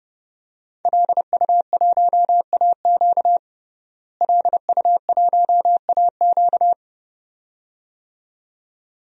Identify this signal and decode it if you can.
Morse code, heard as a keyed tone, about 30 wpm, 700 Hz. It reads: LU1AQ LU1AQ
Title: DXCC-100 Call Signs - 5 Characters - Repeated 2x - 30wpm